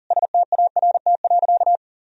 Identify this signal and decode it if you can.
Morse code, heard as a keyed tone, about 40 wpm, 700 Hz. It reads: START.
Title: Sentences from Top 300 Words and Abbreviations 40wpm